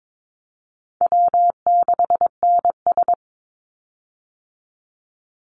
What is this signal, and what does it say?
Morse code, heard as a keyed tone, about 22 wpm, 700 Hz. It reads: W6NS
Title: Call Signs Repeated 3x - 22wpm